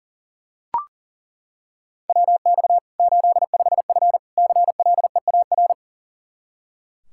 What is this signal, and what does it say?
Morse code, heard as a keyed tone, about 40 wpm, 700 Hz. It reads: WX 85F CLEAR
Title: Practice QSO - Exchange 2 - Info - 40wpm